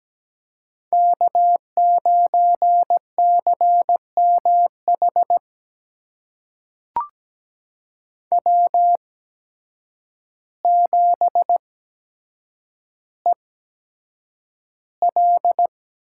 Morse code, heard as a keyed tone, about 17 wpm, spaced as for 8 wpm, 700 Hz. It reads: K9CMH W7EL
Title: USA Call Signs - 5 Characters - 8x Character Spacing - 17wpm